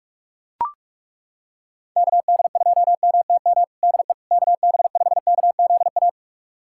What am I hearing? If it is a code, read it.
Morse code, heard as a keyed tone, about 41 wpm, 700 Hz. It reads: KD2MTM DE KB5KZA
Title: USA Callsign DE Callsign - 6 Characters - 45wpm